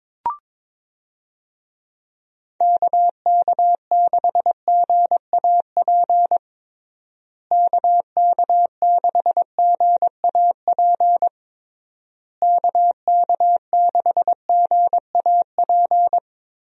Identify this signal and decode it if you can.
Morse code, heard as a keyed tone, about 22 wpm, 700 Hz. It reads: KK6GAP KK6GAP KK6GAP
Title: USA Call Signs - 6 Characters - Repeated 3x - 22wpm